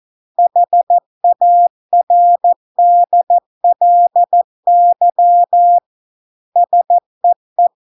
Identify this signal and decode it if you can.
Morse code, heard as a keyed tone, about 14 wpm, 700 Hz. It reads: HARDLY SEE